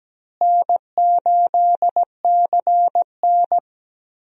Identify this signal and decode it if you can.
Morse code, heard as a keyed tone, about 17 wpm, 700 Hz. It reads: N8CN
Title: USA Call Signs - 4 Characters - Repeated 2x - 17wpm